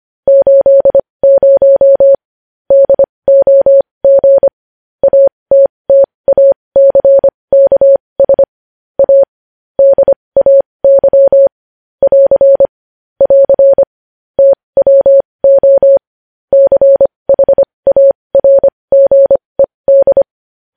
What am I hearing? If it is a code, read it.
Morse code, heard as a keyed tone, about 25 wpm, spaced as for 20 wpm, 550 Hz. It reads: 80 DOG ATTACKS A DAY <AR> <AR> TWO CHARGED
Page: news-headlines-in-morse-code-at-20-wpm